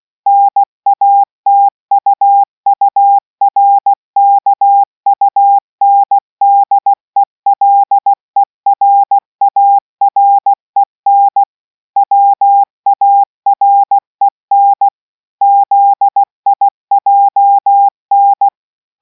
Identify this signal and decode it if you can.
Morse code, heard as a keyed tone, about 16 wpm, 800 Hz. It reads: NATUURKUNDELERAREN WAREN ZIJN